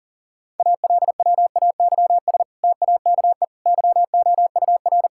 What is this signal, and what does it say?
Morse code, heard as a keyed tone, about 40 wpm, 700 Hz. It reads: ALWAYS TAKE YOUR
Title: Sentences from Top 1000 Words - Review Part 5 - 40wpm